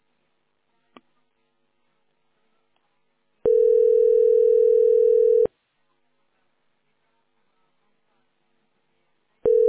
Ringback tone (North America), 440 + 480 Hz.